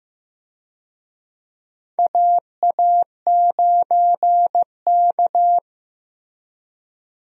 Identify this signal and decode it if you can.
Morse code, heard as a keyed tone, about 16 wpm, 700 Hz. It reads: AA9K
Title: USA Call Signs - 4 Characters - Repeated 2x - 15wpm